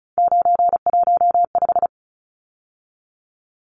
Morse code, heard as a keyed tone, about 34 wpm, 700 Hz. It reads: 915
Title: Numbers - 3-Digit 35wpm